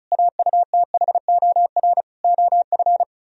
Morse code, heard as a keyed tone, about 46 wpm, 700 Hz. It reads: AUTHOR OF